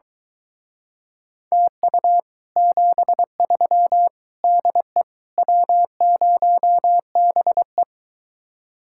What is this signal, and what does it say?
Morse code, heard as a keyed tone, about 23 wpm, 700 Hz. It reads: TU 73 DE W0BE